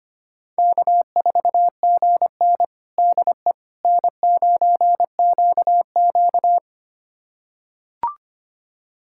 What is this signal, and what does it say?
Morse code, heard as a keyed tone, about 25 wpm, 700 Hz. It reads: K4GN DE N9QQ
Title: USA Callsign DE Callsign - 4 Characters - 25wpm